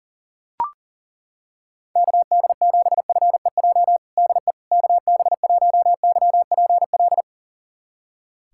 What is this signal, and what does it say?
Morse code, heard as a keyed tone, about 40 wpm, 700 Hz. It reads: KD7FEJ DE KB1YPL
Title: USA Callsign DE Callsign - 6 Characters - 40wpm